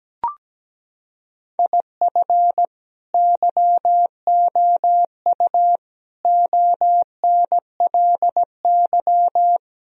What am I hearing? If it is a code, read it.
Morse code, heard as a keyed tone, about 17 wpm, 700 Hz. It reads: IF YOU ONLY
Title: Sentences from Top 1000 Words - Review Part 4 - 17wpm